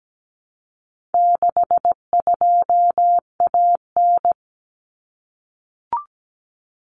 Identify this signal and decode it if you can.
Morse code, heard as a keyed tone, about 17 wpm, 700 Hz. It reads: 62AN
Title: Common Cut Numbers - 4-Digit 17wpm